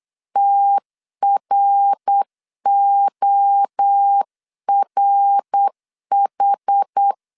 Morse code, heard as a keyed tone, about 8 wpm, 800 Hz. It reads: TRORH